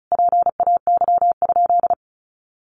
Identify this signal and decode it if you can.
Morse code, heard as a keyed tone, about 34 wpm, 700 Hz. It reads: PAY?
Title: Sentences from Top 700 Words 35wpm